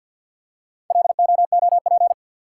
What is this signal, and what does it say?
Morse code, heard as a keyed tone, about 52 wpm, 700 Hz. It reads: LOOP
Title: Top 180 QSO Elements 50wpm